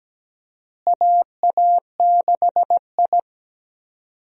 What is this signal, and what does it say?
Morse code, heard as a keyed tone, about 17 wpm, 700 Hz. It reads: AA6I
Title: USA Call Signs - 4 Characters - Repeated 3x - 17wpm